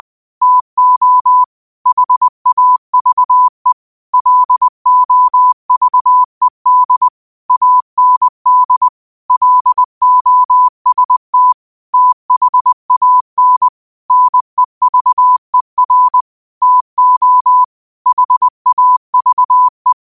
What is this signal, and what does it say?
Morse code, heard as a keyed tone, about 20 wpm, 1000 Hz. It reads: TO HAVE LOVED AND LOST THAN NEVER TO HAVE
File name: qotdfiles / 20wpm